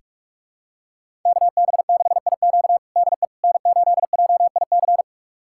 Morse code, heard as a keyed tone, about 46 wpm, 700 Hz. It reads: KB6IQ DE N8JIC
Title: USA Callsign DE Callsign - 5 Characters - 45wpm